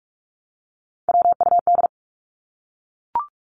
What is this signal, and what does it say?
Morse code, heard as a keyed tone, about 44 wpm, 700 Hz. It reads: WUD
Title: Abbreviations 45wpm